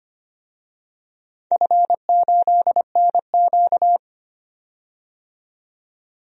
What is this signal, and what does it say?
Morse code, heard as a keyed tone, about 25 wpm, 700 Hz. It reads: F8NQ
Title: DXCC-100 Call Signs - 4 Characters - 25wpm